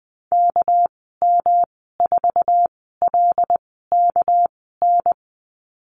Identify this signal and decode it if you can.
Morse code, heard as a keyed tone, about 20 wpm, spaced as for 15 wpm, 700 Hz. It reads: KM4LKN
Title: 200 Lesson 32g - Letter k - Call Signs 20wpm